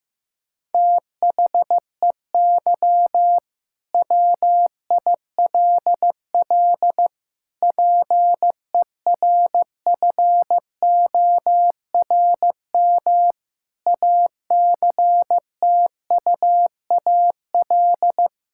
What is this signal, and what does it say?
Morse code, heard as a keyed tone, about 15 wpm, 700 Hz. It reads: THEY WILL PERFORM ACTUAL
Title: Sentences from Top 1000 Words - Review Part 5 - 15wpm